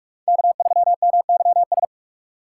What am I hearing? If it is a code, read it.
Morse code, heard as a keyed tone, about 46 wpm, 700 Hz. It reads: K3MYS
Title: USA Call Signs - 5 Characters - 4x Character Spacing - 45wpm